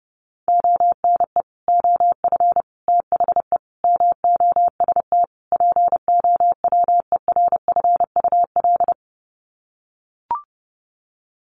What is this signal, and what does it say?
Morse code, heard as a keyed tone, about 30 wpm, 700 Hz. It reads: ONE OF THE MOST POWERFUL
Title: Sets of 5 Words Encore 30wpm